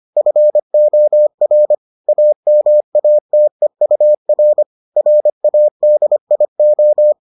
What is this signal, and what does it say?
Morse code, heard as a keyed tone, about 25 wpm, 600 Hz. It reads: FOR AMATEUR RADIO